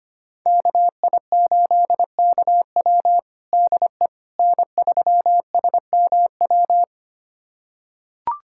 Morse code, heard as a keyed tone, about 25 wpm, 700 Hz. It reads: KI8KW DE N3SMW
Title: USA Callsign DE Callsign - 5 Characters - 25wpm